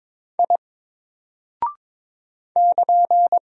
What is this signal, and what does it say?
Morse code, heard as a keyed tone, about 22 wpm, 700 Hz. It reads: I <KN>
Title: Single Character (Letter:Number:Punctuation:Prosign) Alt 22wpm